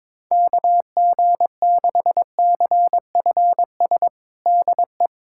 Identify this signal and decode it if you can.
Morse code, heard as a keyed tone, about 22 wpm, 700 Hz. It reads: KG6CFS DE
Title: USA Callsign DE Callsign - 6 Characters - 22wpm